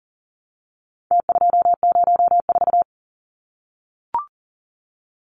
Morse code, heard as a keyed tone, about 40 wpm, 700 Hz. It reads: T204
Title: Numbers - Contest Style - 4-Digit 40wpm